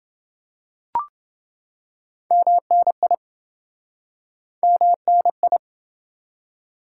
Morse code, heard as a keyed tone, about 28 wpm, 700 Hz. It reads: MNI MNI